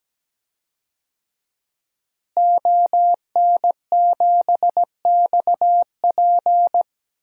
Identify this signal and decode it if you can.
Morse code, heard as a keyed tone, about 17 wpm, 700 Hz. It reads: ON7XP
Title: DXCC-100 Call Signs - 5 Characters - Repeated 2x - 17wpm